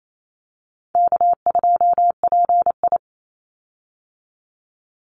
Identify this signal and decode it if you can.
Morse code, heard as a keyed tone, about 28 wpm, 700 Hz. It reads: K2PI
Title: Easy Call Signs - 28wpm